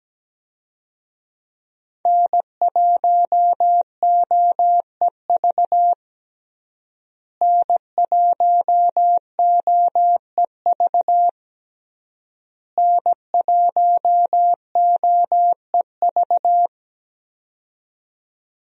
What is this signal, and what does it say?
Morse code, heard as a keyed tone, about 17 wpm, 700 Hz. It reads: N1OEV N1OEV N1OEV
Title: USA Call Signs - 5 Characters - Repeated 3x - 17wpm